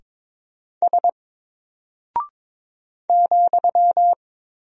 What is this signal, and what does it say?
Morse code, heard as a keyed tone, about 22 wpm, 700 Hz. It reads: S ,
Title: Single Character (Letter:Number:Punctuation:Prosign) Alt 22wpm